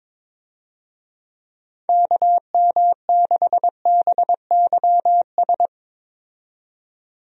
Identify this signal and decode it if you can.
Morse code, heard as a keyed tone, about 21 wpm, 700 Hz. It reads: KM6BYS
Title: USA Call Signs - 6 Characters - 22wpm